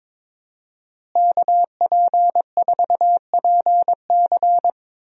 Morse code, heard as a keyed tone, about 22 wpm, 700 Hz. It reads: KP4PC
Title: DXCC-100 Call Signs - 5 Characters - Repeated 3x - 22wpm